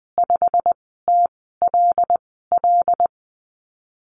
Morse code, heard as a keyed tone, about 20 wpm, 700 Hz. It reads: 5 T L L
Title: 122 Lesson 20g - Number 3 - Call Signs 20wpm